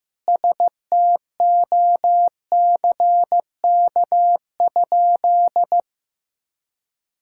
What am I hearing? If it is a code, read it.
Morse code, heard as a keyed tone, about 14 wpm, 700 Hz. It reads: STOCK?
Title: Sentences from Top 1000 Words - Review Part 5 - 15wpm